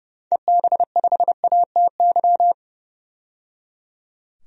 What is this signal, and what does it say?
Morse code, heard as a keyed tone, about 30 wpm, 700 Hz. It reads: EB5ATY
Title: DXCC-100 Call Signs - 6 Characters - 30wpm